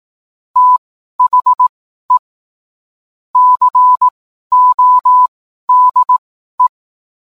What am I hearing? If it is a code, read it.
Morse code, heard as a keyed tone, about 18 wpm, spaced as for 12 wpm, 1000 Hz. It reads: THE CODE